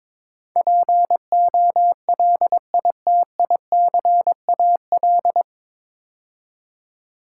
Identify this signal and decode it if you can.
Morse code, heard as a keyed tone, about 22 wpm, 700 Hz. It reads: POLITICAL
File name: 9 Letter Words - First 3 Letters Given - 22wpm